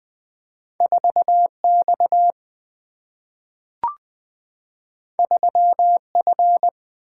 Morse code, heard as a keyed tone, about 20 wpm, 700 Hz. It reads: 4 X 3 F
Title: Two Letter-Number Combinations - 20wpm